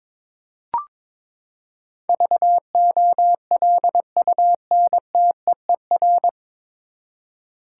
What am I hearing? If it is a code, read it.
Morse code, heard as a keyed tone, about 21 wpm, 700 Hz. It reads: VOLUNTEER
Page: Morse Code Ninja practice 22wpm